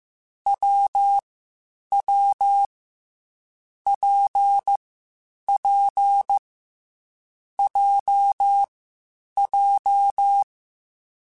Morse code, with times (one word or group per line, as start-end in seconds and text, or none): 0.5-1.2 W
1.9-2.6 W
3.9-4.8 P
5.5-6.4 P
7.6-8.6 J
9.4-10.4 J